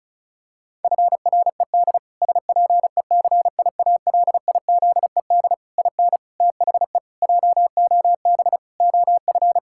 Morse code, heard as a keyed tone, about 34 wpm, 700 Hz. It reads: FRED SPECIALIZED IN THE JOB OF
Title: Pangram Challenge v2 - 35wpm